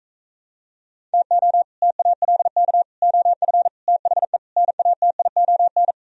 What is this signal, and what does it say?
Morse code, heard as a keyed tone, about 45 wpm, 700 Hz. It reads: TO TALK OF THE NATION